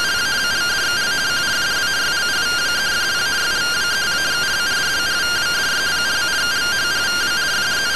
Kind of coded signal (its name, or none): two-tone teleprinter signal